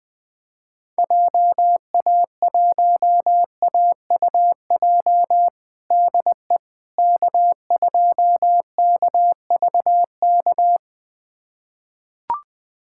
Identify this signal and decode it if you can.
Morse code, heard as a keyed tone, about 21 wpm, 700 Hz. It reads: JA1AUJ DE K2KVK
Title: USA Callsign DE Callsign 20wpm